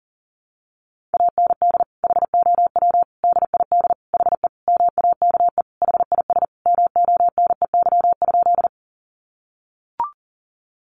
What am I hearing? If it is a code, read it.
Morse code, heard as a keyed tone, about 40 wpm, 700 Hz. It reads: AND HOW DID HE MAKE HIS MONEY?
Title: Sentences from Top 400 Words 40wpm